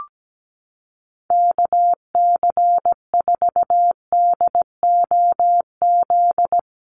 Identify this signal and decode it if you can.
Morse code, heard as a keyed tone, about 17 wpm, 700 Hz. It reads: KC4DOZ
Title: USA Call Signs Repeated 3x 17wpm